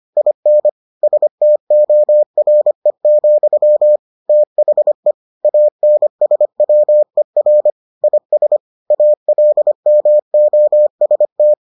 Morse code, heard as a keyed tone, about 25 wpm, 600 Hz. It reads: IN STORE, THE ANSWER IS ALMOST